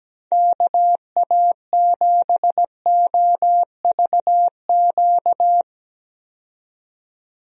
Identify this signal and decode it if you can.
Morse code, heard as a keyed tone, about 17 wpm, 700 Hz. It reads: KA7OVQ
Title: USA Call Signs - 6 Characters - 17wpm